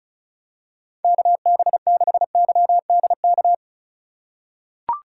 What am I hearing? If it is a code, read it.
Morse code, heard as a keyed tone, about 36 wpm, 700 Hz. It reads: KB6YDK
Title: USA Call Signs - 6 Characters - 3x Character Spacing - 35wpm